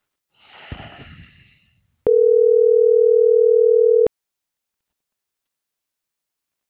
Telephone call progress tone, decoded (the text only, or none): ringback tone